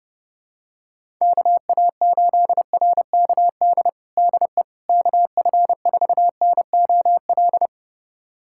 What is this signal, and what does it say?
Morse code, heard as a keyed tone, about 31 wpm, 700 Hz. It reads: KA8RKD DE KF4NOL
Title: USA Callsign DE Callsign - 6 Characters - 30wpm